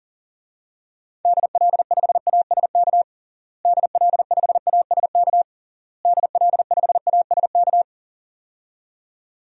Morse code, heard as a keyed tone, about 40 wpm, 700 Hz. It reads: DL5ASK DL5ASK DL5ASK
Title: DXCC-100 Call Signs - 6 Characters - Repeated 3x - 40wpm